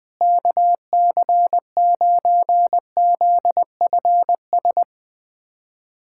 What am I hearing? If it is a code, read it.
Morse code, heard as a keyed tone, about 20 wpm, 700 Hz. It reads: KC9ZFS
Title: USA Callsign DE Callsign - 6 Characters - 20wpm